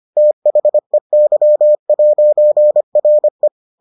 Morse code, heard as a keyed tone, about 25 wpm, 600 Hz. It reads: THEY'RE